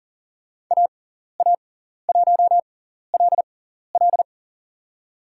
Morse code, heard as a keyed tone, about 40 wpm, spaced as for 14 wpm, 700 Hz. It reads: AA1LL
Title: USA Call Signs - 5 Characters - 6x Character Spacing - 40wpm